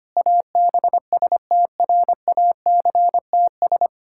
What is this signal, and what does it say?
Morse code, heard as a keyed tone, about 25 wpm, 700 Hz. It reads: ABSTRACTS
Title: Pangram Challenge v2 - 25wpm